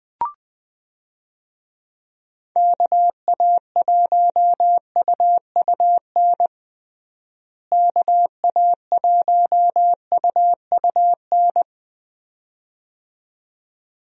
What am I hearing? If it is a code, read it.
Morse code, heard as a keyed tone, about 20 wpm, 700 Hz. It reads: KA1UUN KA1UUN
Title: USA Call Signs - 6 Characters - Repeated 2x - 20wpm